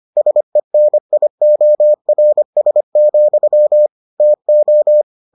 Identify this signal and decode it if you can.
Morse code, heard as a keyed tone, about 25 wpm, 600 Hz. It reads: SENIORS, TO